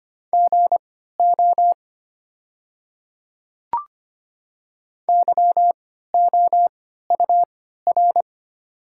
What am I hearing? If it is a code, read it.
Morse code, heard as a keyed tone, about 25 wpm, spaced as for 14 wpm, 700 Hz. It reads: GO YOUR
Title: 3x Character Spacing - Top 100 Words - 25wpm